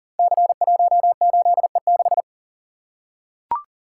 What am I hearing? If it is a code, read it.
Morse code, heard as a keyed tone, about 42 wpm, 700 Hz. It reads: C18E6
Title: Code Groups - 5 Letter:Numbers - 40wpm